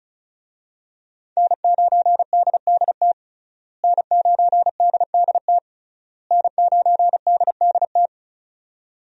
Morse code, heard as a keyed tone, about 36 wpm, 700 Hz. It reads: N9DDT N9DDT N9DDT
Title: USA Call Signs - 5 Characters - Repeated 3x - 35wpm